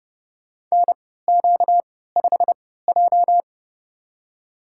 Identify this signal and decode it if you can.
Morse code, heard as a keyed tone, about 32 wpm, spaced as for 17 wpm, 700 Hz. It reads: NQ5J
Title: USA Call Signs - 3x Character Spacing - 30wpm